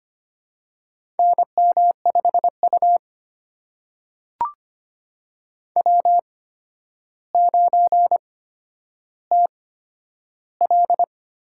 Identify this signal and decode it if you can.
Morse code, heard as a keyed tone, about 25 wpm, spaced as for 11 wpm, 700 Hz. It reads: NM5U W9TL
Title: USA Call Signs - 4 Characters - 8x Character Spacing - 25wpm